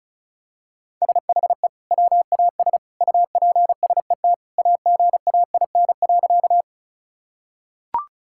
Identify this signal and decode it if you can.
Morse code, heard as a keyed tone, about 36 wpm, 700 Hz. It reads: SHE WAS UPSET AGAIN.
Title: Sentences from Top 1000 Words - Review Part 5 - 35wpm